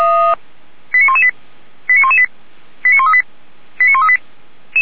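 Five-tone selective call (ZVEI), heard as quick runs of five tones, tones of about 70 ms. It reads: 89118 89118 89138 89138
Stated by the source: ZVEI five-tone selective call received on an SDR.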